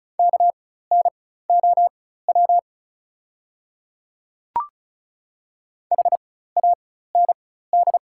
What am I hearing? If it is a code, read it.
Morse code, heard as a keyed tone, about 36 wpm, spaced as for 16 wpm, 690 Hz. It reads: KNOW HAND
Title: 4x Character Spacing - Top 300 Words - 35wpm